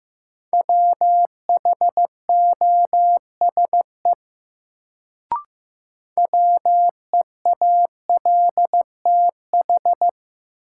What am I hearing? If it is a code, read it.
Morse code, heard as a keyed tone, about 15 wpm, 700 Hz. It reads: WHOSE WEALTH
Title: Sentences from Top 1200 Words - New Word Drill Alt - 15wpm